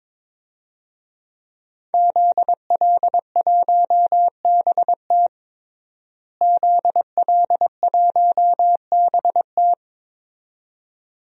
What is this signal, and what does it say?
Morse code, heard as a keyed tone, about 22 wpm, 700 Hz. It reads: ZL1BT ZL1BT
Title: DXCC-100 Call Signs - 5 Characters - Repeated 2x - 22wpm